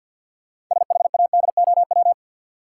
Morse code, heard as a keyed tone, about 50 wpm, 700 Hz. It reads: SHADOW
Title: Sets of 5 Words Encore - New Word Drill - 50wpm